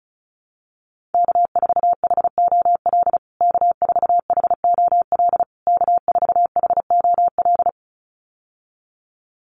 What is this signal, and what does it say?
Morse code, heard as a keyed tone, about 34 wpm, 700 Hz. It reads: K4HOL K4HOL K4HOL
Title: USA Call Signs Repeated 3x 35wpm